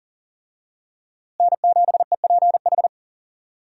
Morse code, heard as a keyed tone, about 40 wpm, 700 Hz. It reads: N7EPH
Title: USA Call Signs - 5 Characters - Repeated 2x - 40wpm